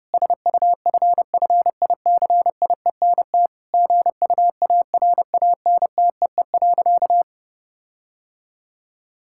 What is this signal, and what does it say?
Morse code, heard as a keyed tone, about 30 wpm, 700 Hz. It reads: SUFFICIENT GUARANTEE.
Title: Sentences from Top 1600 Words 30wpm